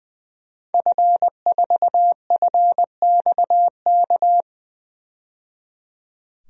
Morse code, heard as a keyed tone, about 20 wpm, 700 Hz. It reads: F4FXK
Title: DXCC-100 Call Signs - 5 Characters - 20wpm